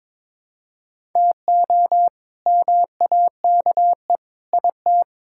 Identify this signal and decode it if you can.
Morse code, heard as a keyed tone, about 22 wpm, 700 Hz. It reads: TO MAKE IT
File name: Sentences with 7-Word Phrases - 22wpm